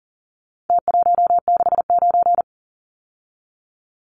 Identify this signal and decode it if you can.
Morse code, heard as a keyed tone, about 40 wpm, 700 Hz. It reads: T169
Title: Numbers - Contest Style - 4-Digit 40wpm